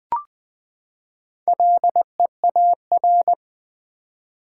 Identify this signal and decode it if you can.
Morse code, heard as a keyed tone, about 20 wpm, 700 Hz. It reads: LEAR
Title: Build A Word - 4 Letters At A Time - 8-Letter Words - 20wpm